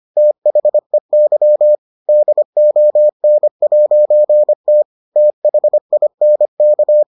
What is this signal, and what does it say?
Morse code, heard as a keyed tone, about 25 wpm, 600 Hz. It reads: THEY DON'T THINK